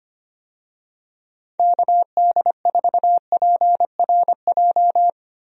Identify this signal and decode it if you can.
Morse code, heard as a keyed tone, about 25 wpm, 700 Hz. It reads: KD4PRJ
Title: USA Call Signs - 6 Characters - Repeated 2x - 25wpm